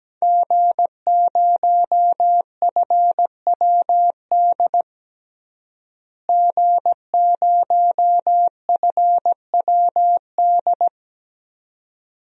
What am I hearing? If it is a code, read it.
Morse code, heard as a keyed tone, about 17 wpm, 700 Hz. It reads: G0FWD G0FWD
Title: DXCC-100 Call Signs - 5 Characters - Repeated 2x - 17wpm